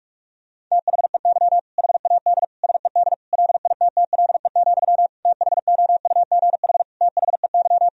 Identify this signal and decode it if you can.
Morse code, heard as a keyed tone, about 41 wpm, 700 Hz. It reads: THEY HAD SED LITTLE, THOUGH THEY
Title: Sentences from Top 400 Words and Abbreviations 45wpm